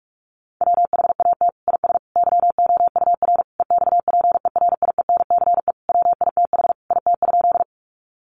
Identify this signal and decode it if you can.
Morse code, heard as a keyed tone, about 46 wpm, 700 Hz. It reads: WHAT IS YOUR EXPERIENCE WITH IT?
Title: Sentences from Top 600 Words 45wpm